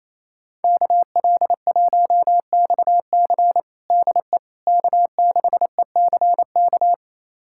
Morse code, heard as a keyed tone, about 28 wpm, 700 Hz. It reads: KL1XC DE K6ECK
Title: USA Callsign DE Callsign - 5 Characters - 28wpm